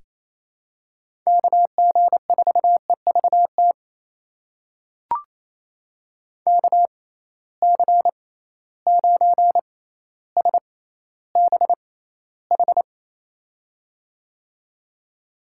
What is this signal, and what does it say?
Morse code, heard as a keyed tone, about 27 wpm, spaced as for 14 wpm, 700 Hz. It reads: KG4EVT KC9SBH
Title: USA Call Signs - 6 Characters - 6x Character Spacing - 28wpm